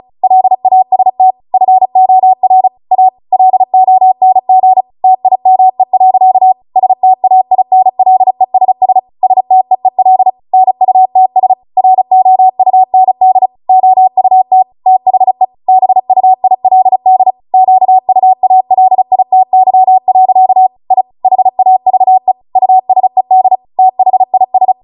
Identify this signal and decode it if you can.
Morse code, heard as a keyed tone, about 36 wpm, 750 Hz. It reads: LAST FOR A LONG TIME. STAINLESS STEEL NUTS ROUND OUT THE BUILD QUALITY. I HAVE USED THIS